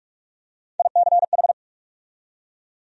Morse code, heard as a keyed tone, about 43 wpm, 700 Hz. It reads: ICH
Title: USA Call Sign Postfixes - 3 Letters 45wpm